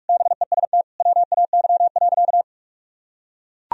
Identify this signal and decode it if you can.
Morse code, heard as a keyed tone, about 41 wpm, 700 Hz. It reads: BEST WAY.
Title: Sentences from Top 400 Words and Abbreviations 45wpm